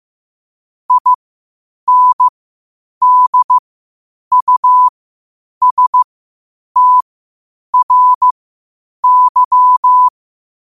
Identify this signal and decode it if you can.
Morse code, heard as a keyed tone, about 15 wpm, spaced as for 8 wpm, 1000 Hz. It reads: INDUSTRY
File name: qotdfiles / 8wpm